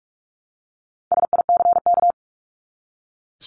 Morse code, heard as a keyed tone, about 44 wpm, 700 Hz. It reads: SICK
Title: Top 1501-1750 Words 45wpm